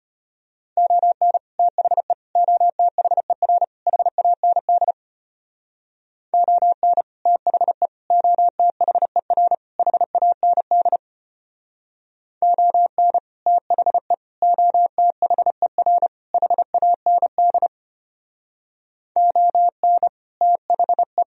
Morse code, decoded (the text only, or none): ON THE OTHER HAND ON THE OTHER HAND ON THE OTHER HAND ON THE